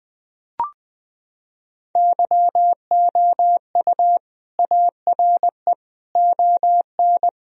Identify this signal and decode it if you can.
Morse code, heard as a keyed tone, about 20 wpm, 700 Hz. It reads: YOU ARE ON
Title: Sentences from Top 1000 Words - Review Part 2 - 20wpm